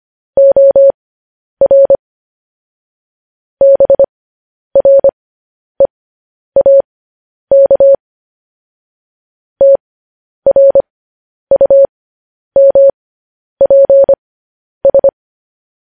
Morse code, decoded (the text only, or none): OR BREAK TRUMPS